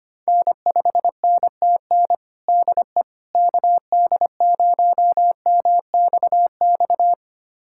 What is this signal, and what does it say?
Morse code, heard as a keyed tone, about 25 wpm, 700 Hz. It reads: N5NTN DE KD0MXX
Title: USA Callsign DE Callsign 25wpm